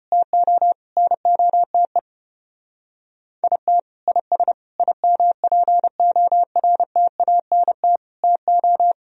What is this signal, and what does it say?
Morse code, heard as a keyed tone, about 33 wpm, 700 Hz. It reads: TO NOTE IT IS IMPORTANT TO